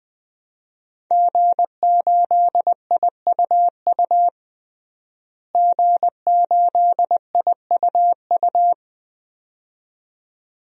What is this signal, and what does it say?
Morse code, heard as a keyed tone, about 20 wpm, 700 Hz. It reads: G8IUU G8IUU
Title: DXCC-100 Call Signs - 5 Characters - Repeated 2x - 20wpm